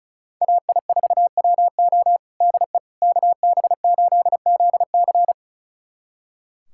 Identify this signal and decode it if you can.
Morse code, heard as a keyed tone, about 34 wpm, 700 Hz. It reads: AI4WO DE KB8ZC
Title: USA Callsign DE Callsign - 5 Characters - 35wpm